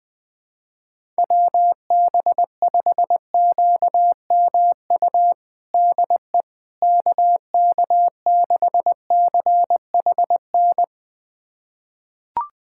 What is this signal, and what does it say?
Morse code, heard as a keyed tone, about 21 wpm, 700 Hz. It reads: WB5QMU DE KK6CHN
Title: USA Callsign DE Callsign 20wpm